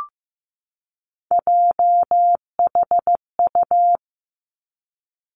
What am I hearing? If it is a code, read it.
Morse code, heard as a keyed tone, about 15 wpm, 700 Hz. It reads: JHU
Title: Three Challenging Letter-Number Combinations 15wpm